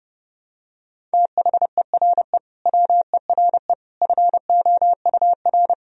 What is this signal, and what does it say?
Morse code, heard as a keyed tone, about 30 wpm, 700 Hz. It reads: THERE WERE FOUR
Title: Sentences from Top 1000 Words - Review Part 1 - 30wpm